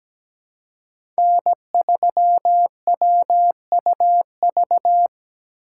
Morse code, heard as a keyed tone, about 17 wpm, 700 Hz. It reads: N3WUV
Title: USA Call Signs - 5 Characters - 3x Character Spacing - 17wpm